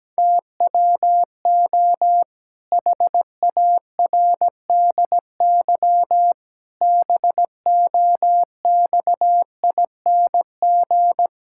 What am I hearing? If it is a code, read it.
Morse code, heard as a keyed tone, about 17 wpm, 700 Hz. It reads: TWO HARDY BOXING
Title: Pangram Challenge v2 - 17wpm